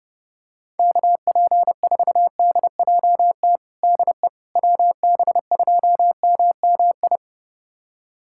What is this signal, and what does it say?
Morse code, heard as a keyed tone, about 30 wpm, 690 Hz. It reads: KP4DJT DE WB2MMI
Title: USA Callsign DE Callsign - 6 Characters - 30wpm